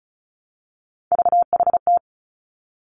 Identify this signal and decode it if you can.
Morse code, heard as a keyed tone, about 36 wpm, 700 Hz. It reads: VHT
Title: Three Challenging Letter-Number Combinations 35wpm